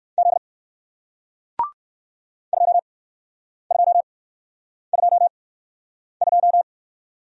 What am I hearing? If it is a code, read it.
Morse code, heard as a keyed tone, about 60 wpm, spaced as for 9 wpm, 700 Hz. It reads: Z 2222